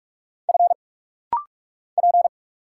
Morse code, heard as a keyed tone, about 46 wpm, 700 Hz. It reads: F P
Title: Single Letters - Rapid-Fire 45wpm